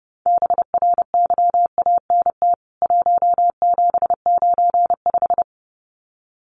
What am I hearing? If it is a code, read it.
Morse code, heard as a keyed tone, about 30 wpm, 700 Hz. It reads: BRYANT 1795
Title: CWT Exchange 30wpm